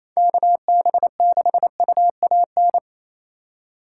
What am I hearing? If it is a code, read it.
Morse code, heard as a keyed tone, about 34 wpm, 700 Hz. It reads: KB6UAN
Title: Speed Racing - USA 2x3 Call Signs 28wpm